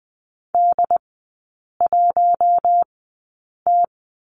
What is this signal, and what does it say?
Morse code, heard as a keyed tone, about 20 wpm, spaced as for 8 wpm, 690 Hz. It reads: D1T
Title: 145 Lesson 24e - Letter y - Three Characters 20wpm